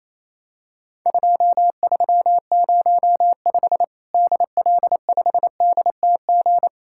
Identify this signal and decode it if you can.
Morse code, heard as a keyed tone, about 28 wpm, 700 Hz. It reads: 2305 DL5DTG
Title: Worked All Europe DX Contest - QTC Exchange - 28wpm